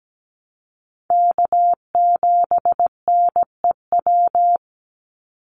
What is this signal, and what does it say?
Morse code, heard as a keyed tone, about 17 wpm, 700 Hz. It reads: K7NEW
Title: NA SOTA Chaser Callsigns 17wpm